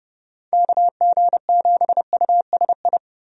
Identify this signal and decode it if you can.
Morse code, heard as a keyed tone, about 30 wpm, 700 Hz. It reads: KG7USI
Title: USA Call Signs - 6 Characters - 30wpm